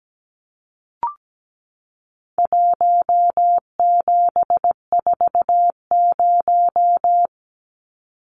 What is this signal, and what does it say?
Morse code, heard as a keyed tone, about 17 wpm, 700 Hz. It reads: 1740
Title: Numbers - 4-Digit 17wpm